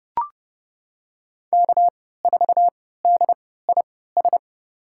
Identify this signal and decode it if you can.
Morse code, heard as a keyed tone, about 31 wpm, spaced as for 17 wpm, 700 Hz. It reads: K4DIS
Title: USA Call Signs - 5 Characters - 3x Character Spacing - 30wpm